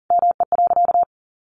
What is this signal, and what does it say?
Morse code, heard as a keyed tone, about 37 wpm, 700 Hz. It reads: ME.
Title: Sentences from Top 1200 Words 40wpm